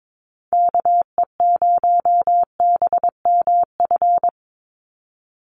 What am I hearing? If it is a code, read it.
Morse code, heard as a keyed tone, about 22 wpm, 700 Hz. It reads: KE0BMF
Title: USA Call Signs 22wpm